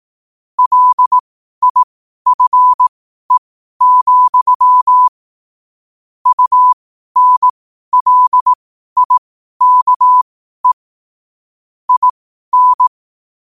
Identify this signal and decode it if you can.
Morse code, heard as a keyed tone, about 18 wpm, spaced as for 12 wpm, 1000 Hz. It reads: LIFE, UNLIKE IN
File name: qotdfiles / 12wpm